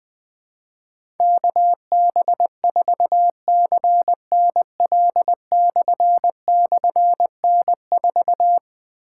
Morse code, heard as a keyed tone, about 20 wpm, 700 Hz. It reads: KB4CNL//N4
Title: Difficult Call Signs 20wpm